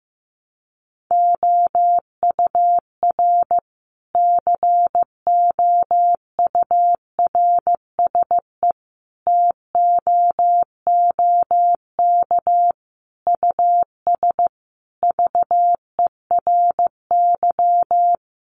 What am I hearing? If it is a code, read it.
Morse code, heard as a keyed tone, about 15 wpm, 700 Hz. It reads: OUR COURSE TOOK US VERY
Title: Sentences from Top 1300 Words 15wpm